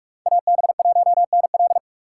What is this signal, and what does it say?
Morse code, heard as a keyed tone, about 45 wpm, 690 Hz. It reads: AB1NL
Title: USA Call Signs - 4x Character Spacing - 45wpm